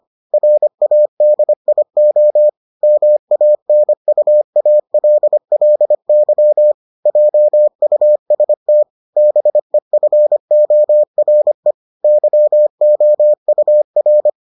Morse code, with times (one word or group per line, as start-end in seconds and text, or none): 0.3-2.5 RADIO
2.8-6.7 MANUALLY
7.1-8.8 JUST
9.2-11.7 BEFORE
12.0-14.3 YOUR